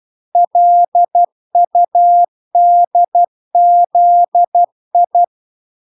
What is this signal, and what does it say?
Morse code, heard as a keyed tone, about 12 wpm, 700 Hz. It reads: LUDZI